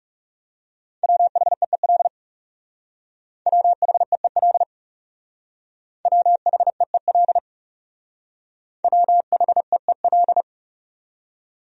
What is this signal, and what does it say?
Morse code, heard as a keyed tone, about 36 wpm, 700 Hz. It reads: WHEEL WHEEL WHEEL WHEEL
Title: Speed-Racing Top 500 Words 30wpm